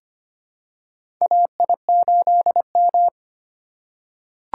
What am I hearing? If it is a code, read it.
Morse code, heard as a keyed tone, about 25 wpm, 700 Hz. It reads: AI8M
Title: USA Call Signs - 4 Characters - 8x Character Spacing - 25wpm